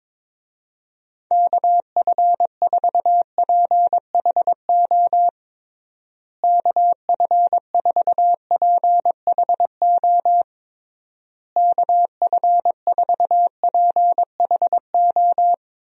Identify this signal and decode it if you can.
Morse code, heard as a keyed tone, about 22 wpm, 700 Hz. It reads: KF4PHO KF4PHO KF4PHO
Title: USA Call Signs - 6 Characters - Repeated 3x - 22wpm